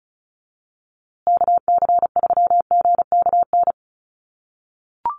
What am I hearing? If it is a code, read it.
Morse code, heard as a keyed tone, about 34 wpm, 700 Hz. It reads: KC3GKN
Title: USA Call Signs Repeated 2x 35wpm